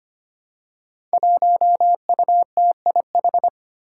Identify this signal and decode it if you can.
Morse code, heard as a keyed tone, about 25 wpm, 700 Hz. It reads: 1UTIH
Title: Five Challenging Letter-Number Combinations 25wpm